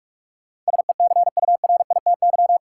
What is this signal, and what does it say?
Morse code, heard as a keyed tone, about 46 wpm, 700 Hz. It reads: SECURITY
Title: Sentences from Top 1400 Words - New Word Drill Alt - 45wpm